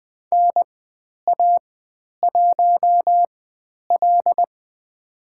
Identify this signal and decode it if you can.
Morse code, heard as a keyed tone, about 20 wpm, spaced as for 10 wpm, 700 Hz. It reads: NA1L
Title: 109 Lesson 18g - Letter m - Call Signs 20wpm